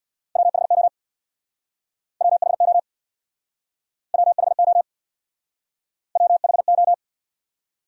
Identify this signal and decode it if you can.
Morse code, heard as a keyed tone, about 67 wpm, 700 Hz. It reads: WHO WHO WHO WHO